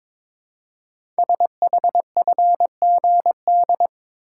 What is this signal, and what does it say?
Morse code, heard as a keyed tone, about 22 wpm, 700 Hz. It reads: SHFGD
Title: Code Groups - 5 Letters - 22wpm